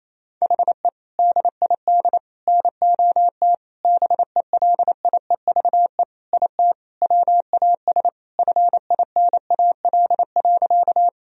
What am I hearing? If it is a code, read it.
Morse code, heard as a keyed tone, about 28 wpm, 700 Hz. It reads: HE DID NOT BELIEVE IT WAS FINAL.
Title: Sentences from Top 1000 Words - Review Part 2 - 28wpm